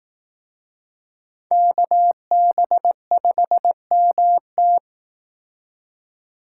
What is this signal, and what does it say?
Morse code, heard as a keyed tone, about 18 wpm, 700 Hz. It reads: KB5MT
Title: Speed Racing - USA 2x2 Call Signs 15wpm